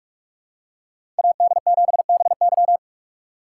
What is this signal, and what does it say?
Morse code, heard as a keyed tone, about 45 wpm, 700 Hz. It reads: AD7BY
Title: USA Call Signs - 2x Character Spacing - 45wpm